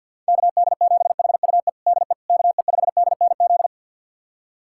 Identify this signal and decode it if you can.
Morse code, heard as a keyed tone, about 52 wpm, 700 Hz. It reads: KD7HUE DE KE5DNZ
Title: USA Callsign DE Callsign - 6 Characters - 50wpm